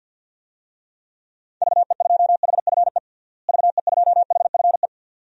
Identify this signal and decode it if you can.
Morse code, heard as a keyed tone, about 52 wpm, 700 Hz. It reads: VE2HFE VE2HFE
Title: DXCC-100 Call Signs - 6 Characters - Repeated 2x - 50wpm